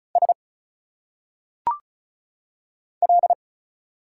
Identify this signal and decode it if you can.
Morse code, heard as a keyed tone, about 36 wpm, 700 Hz. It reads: S L
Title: Single Letters - Phonetic Pronunciation 35wpm